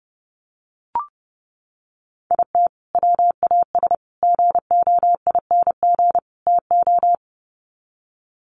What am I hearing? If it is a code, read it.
Morse code, heard as a keyed tone, about 30 wpm, 700 Hz. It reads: IT WAS GOING TO
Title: Sets of 4 Words 30wpm